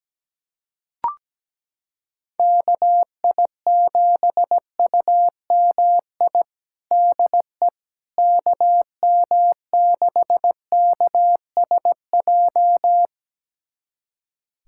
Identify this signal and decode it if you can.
Morse code, heard as a keyed tone, about 17 wpm, 700 Hz. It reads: KI7UMI DE KM6KSJ
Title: USA Callsign DE Callsign - 6 Characters - 17wpm